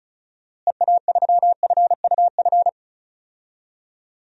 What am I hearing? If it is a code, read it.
Morse code, heard as a keyed tone, about 34 wpm, 700 Hz. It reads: EA3FUF
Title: DXCC-100 Call Signs - 6 Characters - Repeated 3x - 35wpm